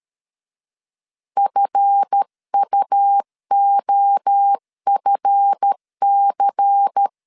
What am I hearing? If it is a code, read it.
Morse code, heard as a keyed tone, about 13 wpm, 800 Hz. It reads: FUOFC
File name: LEZIONI / 7 GF9Z / 7 GF 12wpm 2122